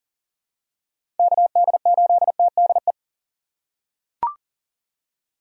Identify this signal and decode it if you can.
Morse code, heard as a keyed tone, about 40 wpm, 700 Hz. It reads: KD8TDE
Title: USA Call Signs - 6 Characters - 4x Character Spacing - 40wpm